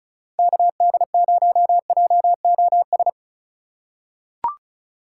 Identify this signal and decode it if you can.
Morse code, heard as a keyed tone, about 36 wpm, 700 Hz. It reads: KD0JOS
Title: USA Call Signs - 4x Character Spacing - 35wpm